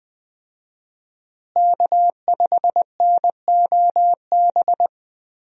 Morse code, heard as a keyed tone, about 20 wpm, 700 Hz. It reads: K5NOB
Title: USA Call Signs - 5 Characters - Repeated 3x - 20wpm